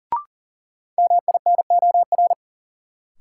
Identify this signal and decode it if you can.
Morse code, heard as a keyed tone, about 40 wpm, 700 Hz. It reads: MINOR
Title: Sentences from Top 1500 Words - New Word Drill Alt - 40wpm